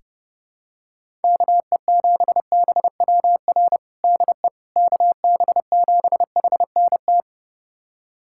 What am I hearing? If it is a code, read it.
Morse code, heard as a keyed tone, about 30 wpm, 700 Hz. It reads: KE7BWR DE KB7HNT
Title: USA Callsign DE Callsign - 6 Characters - 30wpm